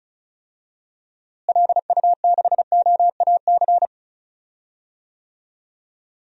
Morse code, heard as a keyed tone, about 34 wpm, 700 Hz. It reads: LU6OAC
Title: DXCC-100 Call Signs - 6 Characters - Repeated 2x - 35wpm